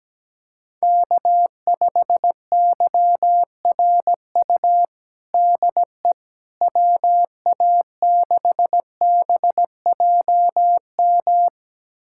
Morse code, heard as a keyed tone, about 17 wpm, 700 Hz. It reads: K5YRU DE WA6BJM
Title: USA Callsign DE Callsign 17wpm